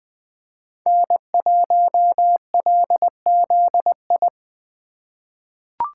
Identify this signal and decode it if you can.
Morse code, heard as a keyed tone, about 20 wpm, 700 Hz. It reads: N1LZI
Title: USA Call Signs - 5 Characters - 4x Character Spacing - 20wpm